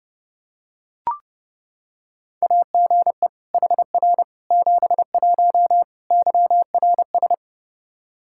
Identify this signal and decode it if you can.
Morse code, heard as a keyed tone, about 32 wpm, 700 Hz. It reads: AGE HR 71 YRS
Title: Practice QSO - Exchange 3 - More Info - 30wpm